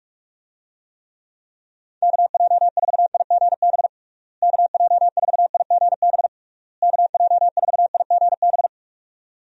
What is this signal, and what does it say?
Morse code, heard as a keyed tone, about 47 wpm, 700 Hz. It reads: KJ4IGB KJ4IGB KJ4IGB
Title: USA Call Signs - 6 Characters - Repeated 3x - 45wpm